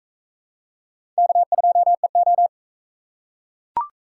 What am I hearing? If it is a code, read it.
Morse code, heard as a keyed tone, about 41 wpm, 700 Hz. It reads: K2EO